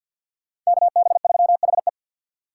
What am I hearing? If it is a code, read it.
Morse code, heard as a keyed tone, about 63 wpm, 700 Hz. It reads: KB3HE